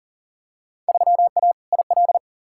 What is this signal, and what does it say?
Morse code, heard as a keyed tone, about 40 wpm, 700 Hz. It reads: 3A IL
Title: Field Day Exchanges - 40wpm